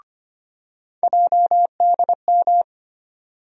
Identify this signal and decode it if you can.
Morse code, heard as a keyed tone, about 25 wpm, 700 Hz. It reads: JDM
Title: USA Call Sign Postfixes - 3 Letters 25wpm